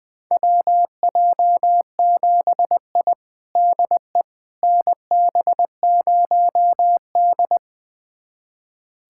Morse code, heard as a keyed tone, about 20 wpm, 700 Hz. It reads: WJ7I DE NB0D
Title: USA Callsign DE Callsign - 4 Characters - 20wpm